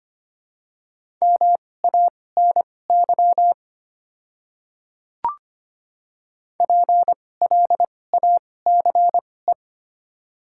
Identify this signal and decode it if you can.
Morse code, heard as a keyed tone, about 25 wpm, spaced as for 18 wpm, 700 Hz. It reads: MANY PLACE
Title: 2x Character Spacing - Top 200 Words - 25wpm